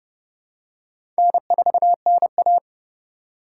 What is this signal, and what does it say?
Morse code, heard as a keyed tone, about 30 wpm, 700 Hz. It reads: N4NA
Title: USA Call Signs - 4 Characters - 4x Character Spacing - 30wpm